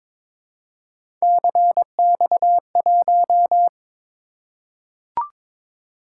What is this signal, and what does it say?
Morse code, heard as a keyed tone, about 22 wpm, 700 Hz. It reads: CX1
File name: practice / 22wpm / Three Letter-Number Combinations - 22wpm